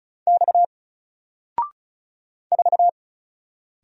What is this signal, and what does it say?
Morse code, heard as a keyed tone, about 35 wpm, 700 Hz. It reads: X 4
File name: Single-Letter-Number-35wpm